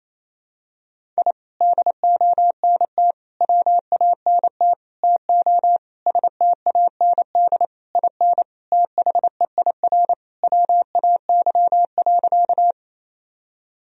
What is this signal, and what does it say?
Morse code, heard as a keyed tone, about 28 wpm, 700 Hz. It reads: I DONT WANT TO STAND IN THEIR WAY.
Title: Sentences from Top 1000 Words - Review Part 1 - 28wpm